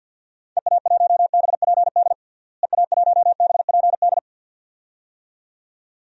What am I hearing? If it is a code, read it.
Morse code, heard as a keyed tone, about 50 wpm, 700 Hz. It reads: EA1BPD EA1BPD
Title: DXCC-100 Call Signs - 6 Characters - Repeated 2x - 50wpm